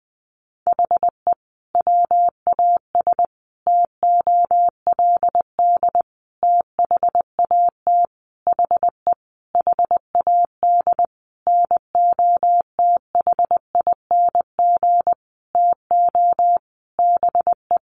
Morse code, heard as a keyed tone, about 20 wpm, 700 Hz. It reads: HE WAS TOLD THAT HE HAD NOTHING TO BE
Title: Sentences from Top 1500 Words 20wpm